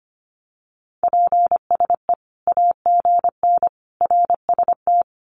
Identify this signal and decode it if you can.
Morse code, heard as a keyed tone, about 25 wpm, 700 Hz. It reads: PSE AGN RST
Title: Common QSO Phrases 25wpm